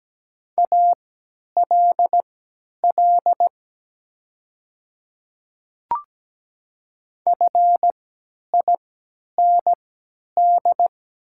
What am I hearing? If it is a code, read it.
Morse code, heard as a keyed tone, about 17 wpm, spaced as for 10 wpm, 700 Hz. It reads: ALL FIND
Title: 3x Character Spacing - Top 100 Words - 17wpm